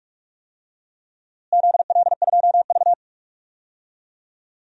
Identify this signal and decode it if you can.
Morse code, heard as a keyed tone, about 47 wpm, 700 Hz. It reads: ZL2V
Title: DXCC-100 Call Signs - 4 Characters - Repeated 2x - 45wpm